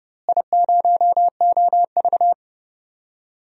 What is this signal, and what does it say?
Morse code, heard as a keyed tone, about 31 wpm, 700 Hz. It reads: I0OV
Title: DXCC-100 Call Signs - 4 Characters - Repeated 2x - 30wpm